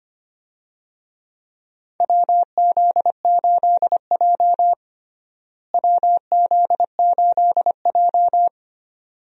Morse code, heard as a keyed tone, about 25 wpm, 700 Hz. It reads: WZ8J WZ8J
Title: USA Call Signs - 4 Characters - Repeated 2x - 25wpm